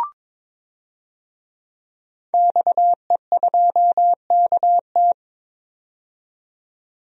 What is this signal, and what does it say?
Morse code, heard as a keyed tone, about 21 wpm, 700 Hz. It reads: XE2KT
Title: DXCC-100 Call Signs - 5 Characters - 22wpm